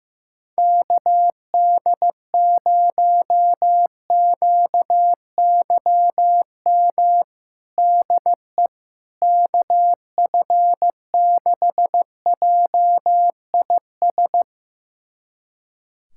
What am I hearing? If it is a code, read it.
Morse code, heard as a keyed tone, about 15 wpm, 700 Hz. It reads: KD0QYM DE KF6JIS
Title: USA Callsign DE Callsign - 6 Characters - 15wpm